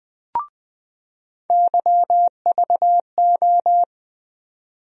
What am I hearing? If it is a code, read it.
Morse code, heard as a keyed tone, about 20 wpm, 700 Hz. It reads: YVO
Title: USA Call Sign Postfixes - 3 Letters 20wpm